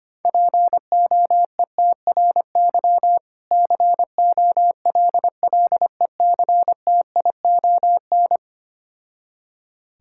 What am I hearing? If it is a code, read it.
Morse code, heard as a keyed tone, about 25 wpm, 700 Hz. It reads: POETRY COLLECTION
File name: practice / 25wpm / Difficult Words with Hints - 25wpm